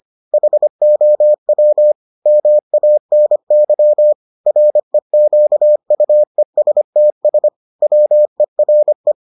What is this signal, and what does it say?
Morse code, heard as a keyed tone, about 25 wpm, 600 Hz. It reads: HOW MANY REQUESTS WERE